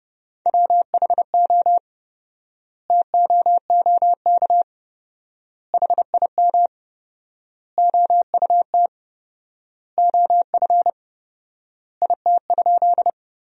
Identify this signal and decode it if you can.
Morse code, heard as a keyed tone, about 32 wpm, 700 Hz. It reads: WHO TOOK HIM OUT OF IT?
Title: 4x Word-Spacing - Sentences from Top 300 Words - 30wpm